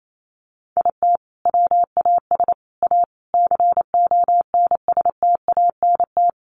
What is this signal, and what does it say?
Morse code, heard as a keyed tone, about 28 wpm, 700 Hz. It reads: IT WAS A CONSTANT
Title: Sentences from Top 1500 Words 28wpm